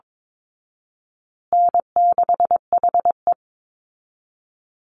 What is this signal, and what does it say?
Morse code, heard as a keyed tone, about 22 wpm, 700 Hz. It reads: N6HE
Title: Easy Call Signs Repeated 3x - 22wpm